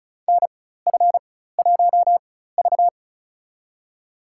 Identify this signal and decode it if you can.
Morse code, heard as a keyed tone, about 36 wpm, spaced as for 16 wpm, 700 Hz. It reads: NF1V
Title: USA Call Signs - 4 Characters - 4x Character Spacing - 35wpm